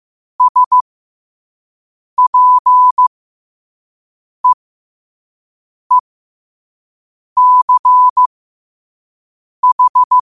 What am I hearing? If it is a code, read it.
Morse code, heard as a keyed tone, about 15 wpm, spaced as for 5 wpm, 1000 Hz. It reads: SPEECH
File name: qotdfiles / 5wpm